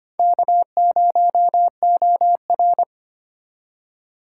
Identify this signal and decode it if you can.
Morse code, heard as a keyed tone, about 31 wpm, 700 Hz. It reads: K0OR
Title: Speed Racing - USA 1x2 Call Signs 25wpm